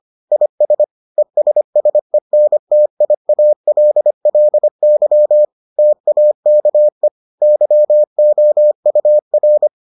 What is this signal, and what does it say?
Morse code, heard as a keyed tone, about 25 wpm, 600 Hz. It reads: IS ESSENTIALLY TAKE YOUR